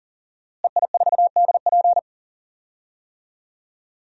Morse code, heard as a keyed tone, about 40 wpm, 700 Hz. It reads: EI4DP
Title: DXCC-100 Call Signs - 5 Characters - Repeated 2x - 40wpm